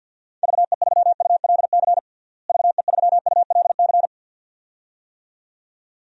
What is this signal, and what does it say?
Morse code, heard as a keyed tone, about 47 wpm, 700 Hz. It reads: VE3ULC VE3ULC
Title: DXCC-100 Call Signs - 6 Characters - Repeated 2x - 50wpm